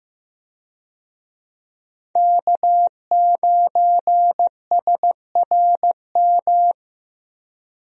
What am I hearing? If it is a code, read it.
Morse code, heard as a keyed tone, about 15 wpm, 700 Hz. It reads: K9SRM
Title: USA Call Signs - 5 Characters - Repeated 2x - 15wpm